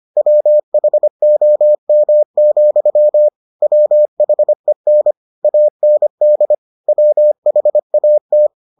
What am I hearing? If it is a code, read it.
Morse code, heard as a keyed tone, about 25 wpm, 600 Hz. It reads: WHOM, WHEN AND WHAT